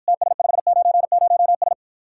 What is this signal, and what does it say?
Morse code, heard as a keyed tone, about 52 wpm, 700 Hz. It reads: TS590S